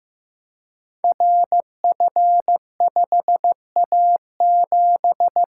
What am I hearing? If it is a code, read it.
Morse code, heard as a keyed tone, about 15 wpm, 700 Hz. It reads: RF5A7
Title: Code Groups - 5 Letter:Numbers - 15wpm